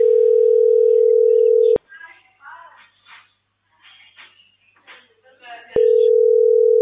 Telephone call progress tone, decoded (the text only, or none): ringback tone